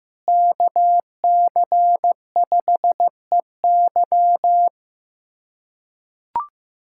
Morse code, heard as a keyed tone, about 15 wpm, 700 Hz. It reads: KC5EY
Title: USA Callsign DE Callsign 15wpm